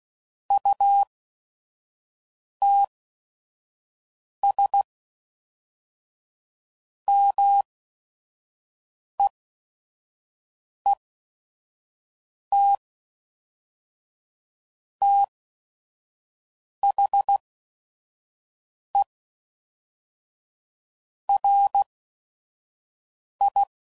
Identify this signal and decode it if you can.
Morse code, heard as a keyed tone, about 16 wpm, spaced as for 5 wpm, 800 Hz. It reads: UTS MEET THE RI
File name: CW Code Groups @5WPM / Technical Subjects / OR Gate